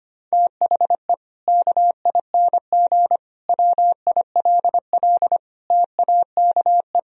Morse code, heard as a keyed tone, about 25 wpm, 700 Hz. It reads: THE KING WILL TAKE
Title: Sentences from Top 1000 Words - Review Part 4 - 25wpm